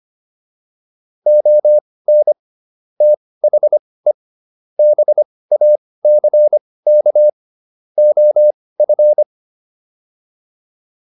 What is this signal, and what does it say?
Morse code, heard as a keyed tone, about 25 wpm, spaced as for 18 wpm, 600 Hz. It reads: ON THE BACK OF